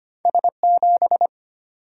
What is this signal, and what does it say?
Morse code, heard as a keyed tone, about 35 wpm, spaced as for 30 wpm, 690 Hz. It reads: S7
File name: practice / 25wpm / Speed-Racing - Two Letter-Number Combinations - 25wpm